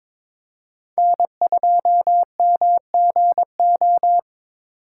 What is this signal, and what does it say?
Morse code, heard as a keyed tone, about 22 wpm, 700 Hz. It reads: N2MGO
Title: USA Call Signs - 5 Characters - 2x Character Spacing - 22wpm